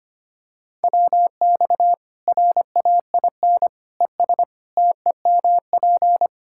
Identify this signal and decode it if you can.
Morse code, heard as a keyed tone, about 25 wpm, 700 Hz. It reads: WX RAIN ES TEMP
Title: Practice QSO - Exchange 2 - Info - 25wpm